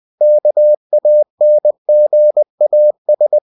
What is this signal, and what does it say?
Morse code, heard as a keyed tone, about 20 wpm, 600 Hz. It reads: KANGAS
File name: cw-books / winnie-the-pooh / winnie-20wpm-page-0069